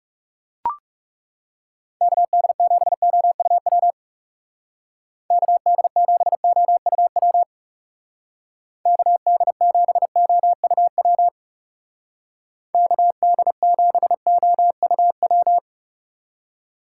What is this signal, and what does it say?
Morse code, heard as a keyed tone, about 38 wpm, 700 Hz. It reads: KD7OUW KD7OUW KD7OUW KD7OUW